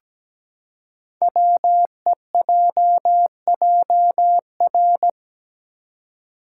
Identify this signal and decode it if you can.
Morse code, heard as a keyed tone, about 17 wpm, 700 Hz. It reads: WEJJR
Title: Code Groups - 5 Letters - 17wpm